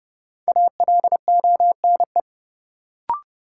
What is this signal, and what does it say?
Morse code, heard as a keyed tone, about 31 wpm, 700 Hz. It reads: ALONE
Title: Sentences from Top 1000 Words - New Word Drill - 30wpm